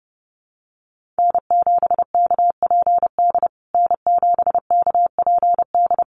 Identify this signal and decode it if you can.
Morse code, heard as a keyed tone, about 30 wpm, 700 Hz. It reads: N7KPD N7KPD
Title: USA Call Signs Repeated 3x 30wpm